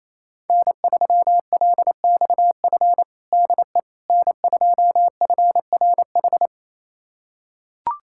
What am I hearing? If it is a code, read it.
Morse code, heard as a keyed tone, about 28 wpm, 700 Hz. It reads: N3LXF DE N2FRH
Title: USA Callsign DE Callsign - 5 Characters - 28wpm